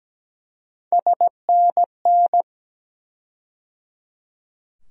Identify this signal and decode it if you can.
Morse code, heard as a keyed tone, about 17 wpm, 700 Hz. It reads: SNN
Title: USA Call Sign Postfixes - 3 Letters 17wpm